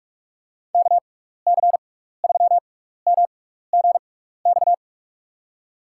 Morse code, heard as a keyed tone, about 47 wpm, spaced as for 16 wpm, 700 Hz. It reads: KC3MGX
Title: USA Call Signs - 6 Characters - 6x Character Spacing - 45wpm